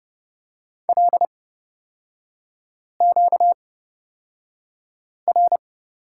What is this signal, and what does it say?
Morse code, heard as a keyed tone, about 31 wpm, spaced as for 5 wpm, 700 Hz. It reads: LQR
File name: Lesson 38e - Letter q - Three Characters 30wpm